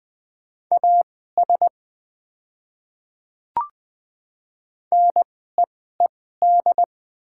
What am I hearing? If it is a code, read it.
Morse code, heard as a keyed tone, about 20 wpm, spaced as for 14 wpm, 700 Hz. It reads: AS NEED
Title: 2x Character Spacing - Top 200 Words - 20wpm